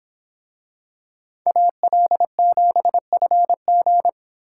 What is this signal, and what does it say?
Morse code, heard as a keyed tone, about 26 wpm, 700 Hz. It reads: AL7FG